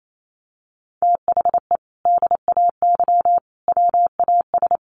Morse code, heard as a keyed tone, about 28 wpm, 700 Hz. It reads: THE DAY WAS
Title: Sentences from Top 1200 Words 28wpm